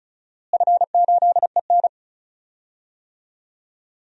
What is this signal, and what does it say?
Morse code, heard as a keyed tone, about 36 wpm, 700 Hz. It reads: F8EN
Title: DXCC-100 Call Signs - 4 Characters - Repeated 2x - 35wpm